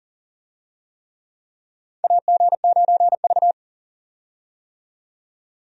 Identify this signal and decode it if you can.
Morse code, heard as a keyed tone, about 40 wpm, 700 Hz. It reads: AG9V
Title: USA Call Signs - 4 Characters - 40wpm